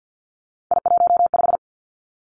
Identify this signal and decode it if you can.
Morse code, heard as a keyed tone, about 50 wpm, 700 Hz. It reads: I15
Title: Three Challenging Letter-Number Combinations 50wpm